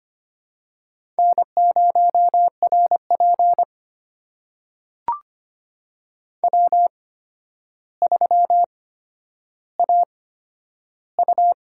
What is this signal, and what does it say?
Morse code, heard as a keyed tone, about 25 wpm, spaced as for 11 wpm, 700 Hz. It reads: N0RP W3AU
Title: USA Call Signs - 4 Characters - 8x Character Spacing - 25wpm